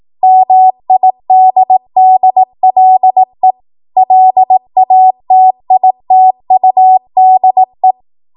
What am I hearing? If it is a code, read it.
Morse code, heard as a keyed tone, about 18 wpm, 750 Hz. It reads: MIDDLE LATITUDE
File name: Archive / 18 WPM / 18WPM